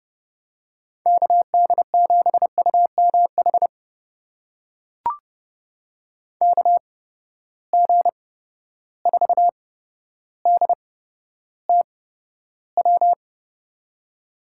Morse code, heard as a keyed tone, about 31 wpm, spaced as for 13 wpm, 700 Hz. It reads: KD7UMH KG4DTW
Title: USA Call Signs - 6 Characters - 8x Character Spacing - 30wpm